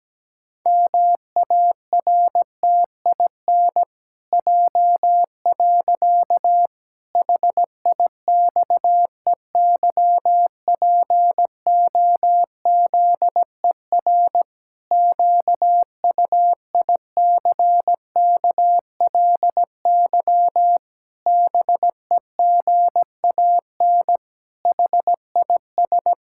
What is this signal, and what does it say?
Morse code, heard as a keyed tone, about 17 wpm, 700 Hz. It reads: MARTIN J. HIXEYPOZER QUICKLY BEGAN HIS
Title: Pangram Challenge v2 - 17wpm